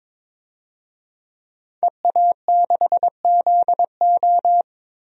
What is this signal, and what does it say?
Morse code, heard as a keyed tone, about 22 wpm, 700 Hz. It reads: EA6ZO
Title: DXCC-100 Call Signs - 5 Characters - Repeated 3x - 22wpm